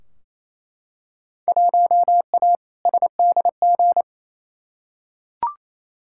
Morse code, heard as a keyed tone, about 28 wpm, 700 Hz. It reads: 1A SDG
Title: Field Day Exchanges Encore - 28wpm